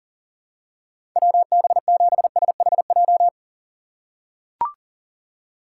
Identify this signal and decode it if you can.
Morse code, heard as a keyed tone, about 37 wpm, 700 Hz. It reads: WB7SHJ
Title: USA Call Signs - 6 Characters - 2x Character Spacing - 40wpm